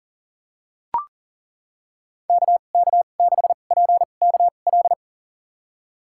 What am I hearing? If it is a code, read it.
Morse code, heard as a keyed tone, about 40 wpm, spaced as for 29 wpm, 700 Hz. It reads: KK6PKL
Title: USA Call Signs - 6 Characters - 2x Character Spacing - 40wpm